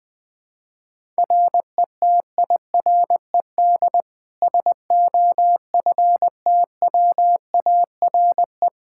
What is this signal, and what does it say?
Morse code, heard as a keyed tone, about 21 wpm, 700 Hz. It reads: RETIRED SOFTWARE
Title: Practice QSO - Exchange 3 - More Info - 20wpm